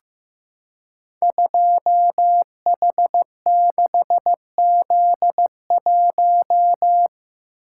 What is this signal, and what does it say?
Morse code, heard as a keyed tone, about 15 wpm, 700 Hz. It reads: 2H6Z1
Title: Code Groups - 5 Letter:Numbers - 15wpm